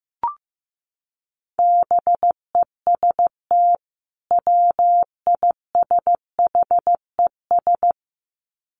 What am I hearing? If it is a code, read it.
Morse code, heard as a keyed tone, about 15 wpm, 700 Hz. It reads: BEST WISHES
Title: Common QSO Phrases 15wpm